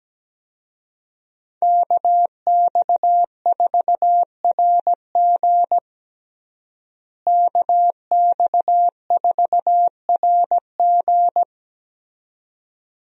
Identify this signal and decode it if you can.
Morse code, heard as a keyed tone, about 17 wpm, 700 Hz. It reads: KX4RG KX4RG
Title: USA Call Signs - 5 Characters - Repeated 2x - 17wpm